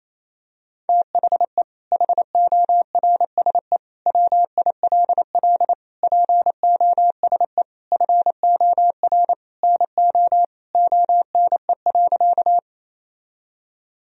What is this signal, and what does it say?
Morse code, heard as a keyed tone, about 28 wpm, 700 Hz. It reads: THE HORSE WILL POSE FOR NO ONE.
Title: Sentences from Top 1000 Words - Review Part 2 - 28wpm